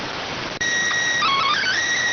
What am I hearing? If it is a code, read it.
SSTV (slow-scan television), PD 290.